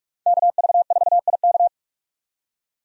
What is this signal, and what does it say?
Morse code, heard as a keyed tone, about 46 wpm, 700 Hz. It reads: KV4IK
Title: USA Call Signs - 5 Characters - 45wpm